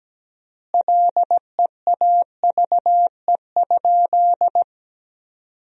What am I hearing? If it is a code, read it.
Morse code, heard as a keyed tone, about 17 wpm, 700 Hz. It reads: LEAVE?
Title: 3x Word-Spacing - Sentences from Top 400 Words - 17wpm